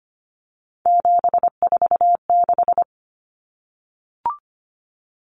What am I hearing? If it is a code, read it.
Morse code, heard as a keyed tone, about 25 wpm, 700 Hz. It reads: 746
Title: Common Cut Numbers - 3-Digit 25wpm